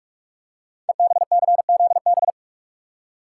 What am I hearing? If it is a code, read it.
Morse code, heard as a keyed tone, about 44 wpm, 700 Hz. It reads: EBCZB
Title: Code Groups - 5 Letters - 45wpm